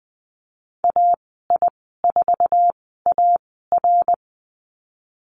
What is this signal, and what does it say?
Morse code, heard as a keyed tone, about 20 wpm, spaced as for 14 wpm, 700 Hz. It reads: AI4AR
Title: 069 Lesson 12g - Letter d - Call Signs 20wpm